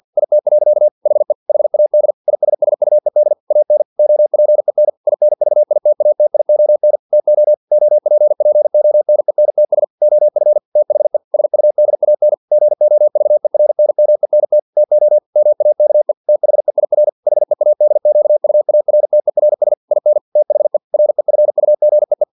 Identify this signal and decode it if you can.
Morse code, heard as a keyed tone, about 50 wpm, 600 Hz. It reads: IT. HE HAD ISSUED AN OPEN INVITATION TO OPPONENTS OF THE SUDAN GOVERNMENT TO MAKE THEIR HEADQUARTERS IN THE REFUGEE
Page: Never 49wpm